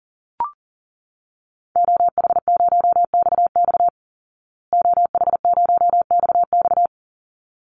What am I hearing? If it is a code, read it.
Morse code, heard as a keyed tone, about 40 wpm, 700 Hz. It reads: OH0XX OH0XX
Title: Call Signs Repeated 3x - 40wpm